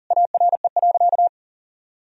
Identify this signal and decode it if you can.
Morse code, heard as a keyed tone, about 37 wpm, 700 Hz. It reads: ARE.
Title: Sentences from Top 1000 Words - Review Part 4 - 40wpm